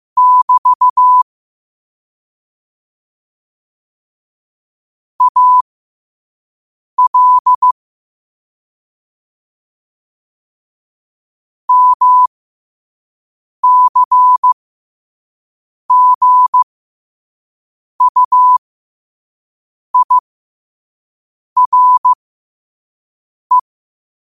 Morse code, heard as a keyed tone, about 15 wpm, spaced as for 5 wpm, 1000 Hz. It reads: <BT> AL MCGUIRE
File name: qotdfiles / 5wpm